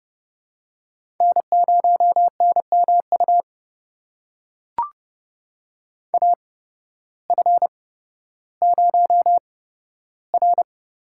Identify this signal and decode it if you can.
Morse code, heard as a keyed tone, about 30 wpm, spaced as for 14 wpm, 700 Hz. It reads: N0NMU AF0R
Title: USA Call Signs - 8x Character Spacing - 30wpm